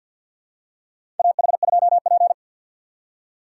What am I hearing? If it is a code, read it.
Morse code, heard as a keyed tone, about 51 wpm, 700 Hz. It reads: AH2P
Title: USA Call Signs - 4 Characters - 6x Character Spacing - 50wpm